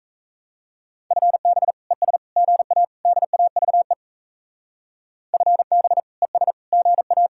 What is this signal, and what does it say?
Morse code, heard as a keyed tone, about 37 wpm, 700 Hz. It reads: FB ES GA DAVE FB ES GA